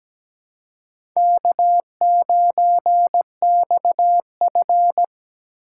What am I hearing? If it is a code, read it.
Morse code, heard as a keyed tone, about 17 wpm, 700 Hz. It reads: K9XF
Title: USA Call Signs - 4 Characters - 8x Character Spacing - 17wpm